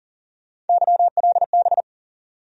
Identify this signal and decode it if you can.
Morse code, heard as a keyed tone, about 40 wpm, 700 Hz. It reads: YLB
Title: USA Call Sign Postfixes - 3 Letters 40wpm